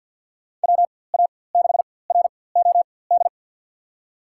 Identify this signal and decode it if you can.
Morse code, heard as a keyed tone, about 52 wpm, spaced as for 23 wpm, 700 Hz. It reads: WA6ROD
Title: USA Call Signs - 4x Character Spacing - 50wpm